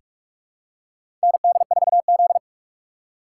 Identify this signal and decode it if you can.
Morse code, heard as a keyed tone, about 46 wpm, 700 Hz. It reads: ND4Z
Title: USA Call Signs - 4 Characters - 6x Character Spacing - 45wpm